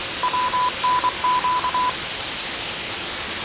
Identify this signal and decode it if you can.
Morse code, heard as a keyed tone, about 24 wpm, 1000 Hz, with noise. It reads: WNQ
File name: 6824usb